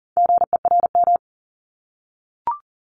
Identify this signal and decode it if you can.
Morse code, heard as a keyed tone, about 41 wpm, 700 Hz. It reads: GERM
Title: Multi-letter Phonemes 40wpm